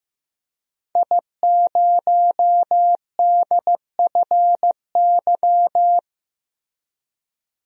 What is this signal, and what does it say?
Morse code, heard as a keyed tone, about 15 wpm, 700 Hz. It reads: I0DFY
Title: DXCC-100 Call Signs - 5 Characters - Repeated 3x - 15wpm